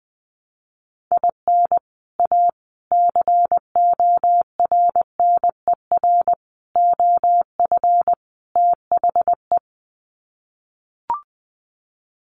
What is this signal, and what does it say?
Morse code, heard as a keyed tone, about 20 wpm, 700 Hz. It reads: IN A CORNER OF THE
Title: Sets of 5 Words Encore2 20wpm